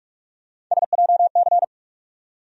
Morse code, heard as a keyed tone, about 46 wpm, 700 Hz. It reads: SJC
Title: USA Call Sign Postfixes - 3 Letters 45wpm